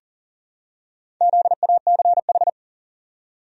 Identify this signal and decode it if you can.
Morse code, heard as a keyed tone, about 40 wpm, 700 Hz. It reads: ZACH
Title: USA Common Names - Max 4 Letters - 40wpm